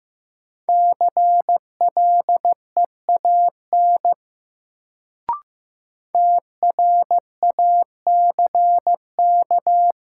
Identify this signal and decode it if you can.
Morse code, heard as a keyed tone, about 15 wpm, 700 Hz. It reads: CLEAN TRACK
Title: Sentences from Top 1100 Words - New Word Drill Alt - 15wpm